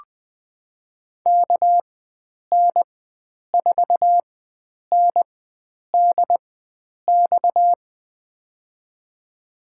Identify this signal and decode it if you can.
Morse code, heard as a keyed tone, about 21 wpm, spaced as for 9 wpm, 700 Hz. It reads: KN4NDX
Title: USA Call Signs - 6 Characters - 4x Character Spacing - 20wpm